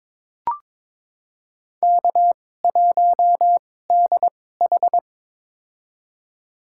Morse code, heard as a keyed tone, about 22 wpm, spaced as for 16 wpm, 700 Hz. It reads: K1DH
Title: USA Call Signs - 2x Character Spacing - 22wpm